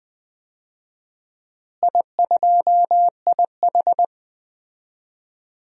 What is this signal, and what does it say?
Morse code, heard as a keyed tone, about 20 wpm, 700 Hz. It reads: I2IH
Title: DXCC-100 Call Signs - 4 Characters - Repeated 2x - 20wpm